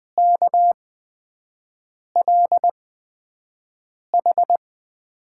Morse code, heard as a keyed tone, about 20 wpm, spaced as for 5 wpm, 700 Hz. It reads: KLH
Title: USA Call Signs - 8x Character Spacing - 20wpm